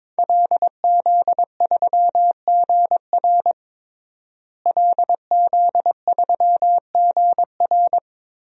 Morse code, heard as a keyed tone, about 22 wpm, 700 Hz. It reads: LZ3GR LZ3GR
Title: DXCC-100 Call Signs - 5 Characters - Repeated 3x - 22wpm